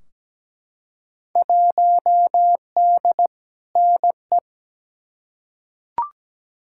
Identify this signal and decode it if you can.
Morse code, heard as a keyed tone, about 17 wpm, 700 Hz. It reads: 1D NE
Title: Field Day Exchanges - 17wpm